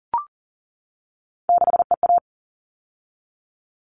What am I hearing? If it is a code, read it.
Morse code, heard as a keyed tone, about 40 wpm, 700 Hz. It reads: 6EA
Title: All Cut Numbers - 3-Digit 40wpm